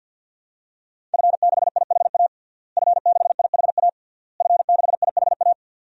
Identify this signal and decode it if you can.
Morse code, heard as a keyed tone, about 53 wpm, 700 Hz. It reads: F6IHA F6IHA F6IHA
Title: DXCC-100 Call Signs - 5 Characters - Repeated 3x - 50wpm